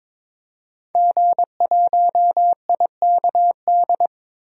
Morse code, heard as a keyed tone, about 22 wpm, 700 Hz. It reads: G1IKD
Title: DXCC-100 Call Signs - 5 Characters - 22wpm